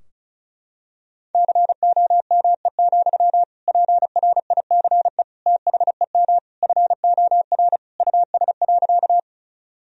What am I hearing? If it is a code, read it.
Morse code, heard as a keyed tone, about 36 wpm, 700 Hz. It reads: COME, PRICE THEM FOR US.
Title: Sentences from Top 1000 Words - Review Part 4 - 35wpm